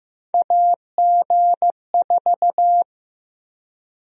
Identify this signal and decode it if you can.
Morse code, heard as a keyed tone, about 15 wpm, 700 Hz. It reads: AG4
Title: USA Call Sign Prefixes - 15wpm